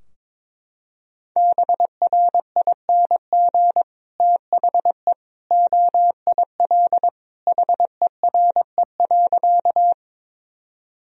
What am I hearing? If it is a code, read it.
Morse code, heard as a keyed tone, about 22 wpm, 700 Hz. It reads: BRING THE OIL HERE.
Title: Sentences from Top 1000 Words - Review Part 4 - 22wpm